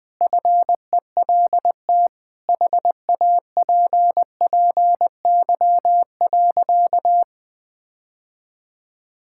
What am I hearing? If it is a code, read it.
Morse code, heard as a keyed tone, about 21 wpm, 700 Hz. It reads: FELT HAPPY.
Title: Sentences with 8-Word Phrases - 20wpm